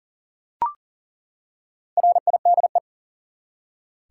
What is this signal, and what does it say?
Morse code, heard as a keyed tone, about 40 wpm, 700 Hz. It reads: RIDE
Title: Four Letter Words - 40wpm